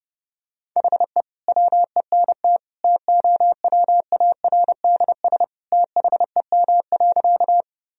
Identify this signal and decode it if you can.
Morse code, heard as a keyed tone, about 32 wpm, 700 Hz. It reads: HE WENT TOWARDS THEM.
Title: Sentences from Top 1000 Words - Review Part 2 - 30wpm